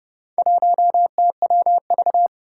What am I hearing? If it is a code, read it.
Morse code, heard as a keyed tone, about 30 wpm, 700 Hz. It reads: 1TWV
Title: Four Challenging Letter-Number Combinations 30wpm